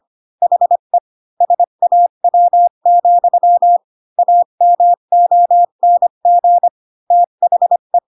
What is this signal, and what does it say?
Morse code, heard as a keyed tone, about 25 wpm, 700 Hz. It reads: HE SAW, AMONG THE
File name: Pinocchio / Pinocchio ch5 20 25 ik1hge